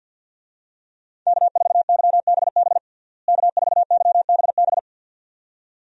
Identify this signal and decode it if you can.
Morse code, heard as a keyed tone, about 52 wpm, 700 Hz. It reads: K4YBB K4YBB
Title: USA Call Signs - 5 Characters - Repeated 2x - 50wpm